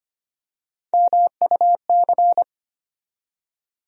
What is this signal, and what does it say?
Morse code, heard as a keyed tone, about 25 wpm, 700 Hz. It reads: MUC
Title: USA Call Sign Postfixes - 3 Letters 25wpm